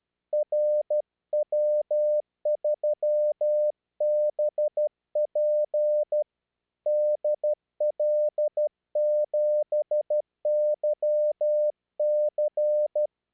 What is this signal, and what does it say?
Morse code, heard as a keyed tone, about 12 wpm, 600 Hz. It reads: RW3BP DL7YC
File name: eme / rw3bp